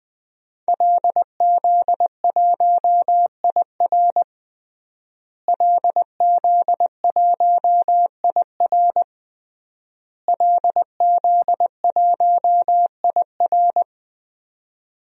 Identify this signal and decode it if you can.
Morse code, heard as a keyed tone, about 20 wpm, 700 Hz. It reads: LZ1IR LZ1IR LZ1IR
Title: DXCC-100 Call Signs - 5 Characters - Repeated 3x - 20wpm